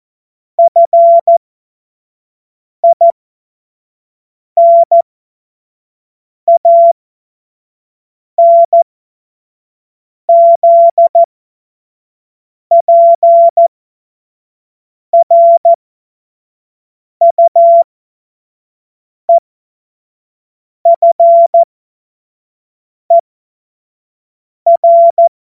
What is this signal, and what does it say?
Morse code, heard as a keyed tone, about 14 wpm, spaced as for 5 wpm, 680 Hz. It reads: FINANZPRUEFER
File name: Morsekurs / Witze / Witz 10. Januar 2023 25 F70